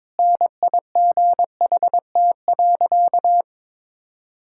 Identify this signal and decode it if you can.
Morse code, heard as a keyed tone, about 22 wpm, 700 Hz. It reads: NIGHT.
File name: practice / 22wpm / Sentences with 2-Word Phrases - 22wpm